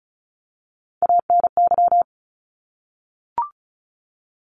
Morse code, heard as a keyed tone, about 34 wpm, 700 Hz. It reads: ANY
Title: Top 200 Words Repeated 35wpm